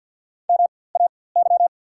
Morse code, heard as a keyed tone, about 54 wpm, spaced as for 24 wpm, 690 Hz. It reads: MAY
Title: 4x Character Spacing - Top 100 Words - 50wpm